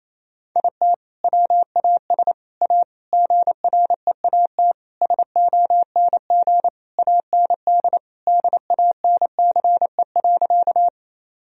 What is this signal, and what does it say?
Morse code, heard as a keyed tone, about 28 wpm, 700 Hz. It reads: IT WAS A GREAT SONG AND DANCE.
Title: Sentences from Top 1000 Words - Review Part 4 - 28wpm